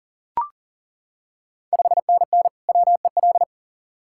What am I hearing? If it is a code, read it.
Morse code, heard as a keyed tone, about 38 wpm, 690 Hz. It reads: 5NN WEL
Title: Ontario QSO Party Exchange 40wpm